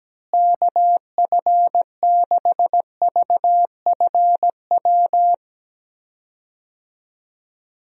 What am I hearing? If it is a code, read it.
Morse code, heard as a keyed tone, about 17 wpm, 700 Hz. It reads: KF6VFW
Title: USA Call Signs - 6 Characters - Repeated 3x - 17wpm